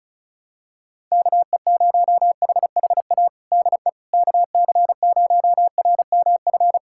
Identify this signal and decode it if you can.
Morse code, heard as a keyed tone, about 36 wpm, 700 Hz. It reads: KE0HHA DE KC0RMF
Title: USA Callsign DE Callsign 35wpm